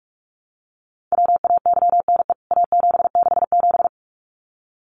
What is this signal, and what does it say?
Morse code, heard as a keyed tone, about 44 wpm, 700 Hz. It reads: WAYNE A767
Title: CWT Exchange 45wpm